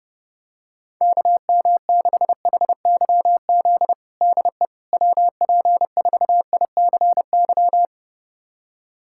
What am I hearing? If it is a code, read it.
Morse code, heard as a keyed tone, about 29 wpm, 700 Hz. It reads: KM6HYZ DE WP4ICY
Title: USA Callsign DE Callsign - 6 Characters - 30wpm